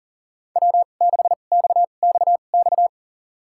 Morse code, heard as a keyed tone, about 40 wpm, spaced as for 29 wpm, 700 Hz. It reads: W6XXX
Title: USA Call Signs - 5 Characters - 2x Character Spacing - 40wpm